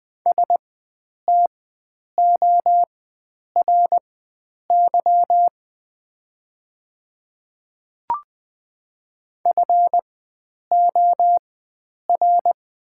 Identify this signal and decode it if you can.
Morse code, heard as a keyed tone, about 21 wpm, spaced as for 9 wpm, 700 Hz. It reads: STORY FOR
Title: 4x Character Spacing - Top 300 Words - 20wpm